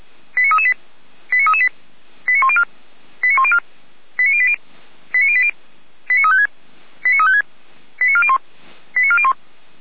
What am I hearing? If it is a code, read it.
Five-tone selective call (ZVEI), heard as quick runs of five tones, tones of about 70 ms. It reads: 89338 89338 89104 89104 89080 89080 89356 89356 89401 89401